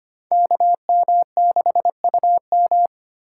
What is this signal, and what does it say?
Morse code, heard as a keyed tone, about 33 wpm, 700 Hz. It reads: KM6UM